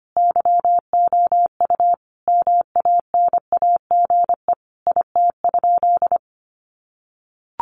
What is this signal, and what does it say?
Morse code, heard as a keyed tone, about 25 wpm, 700 Hz. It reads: YOU MANAGE IT?
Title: Sentences from Top 1400 Words 25wpm